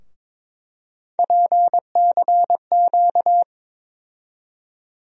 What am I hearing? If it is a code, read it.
Morse code, heard as a keyed tone, about 21 wpm, 700 Hz. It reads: PCQ
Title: USA Call Sign Postfixes - 3 Letters 22wpm